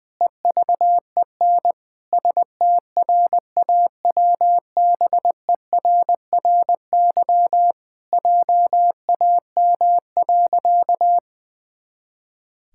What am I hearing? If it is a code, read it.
Morse code, heard as a keyed tone, about 20 wpm, 700 Hz. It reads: EVEN STRAWBERRY JAM.
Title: Pangram Challenge v2 - 20wpm